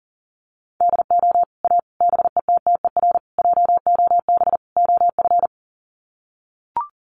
Morse code, heard as a keyed tone, about 40 wpm, 700 Hz. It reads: DO A BETTER JOB OF
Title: Sets of 5 Words Encore2 40wpm